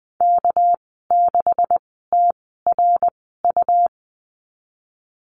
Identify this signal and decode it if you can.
Morse code, heard as a keyed tone, about 20 wpm, spaced as for 15 wpm, 700 Hz. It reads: K6TRU
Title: 213 Lesson 34g - Number 8 - Call Signs 20wpm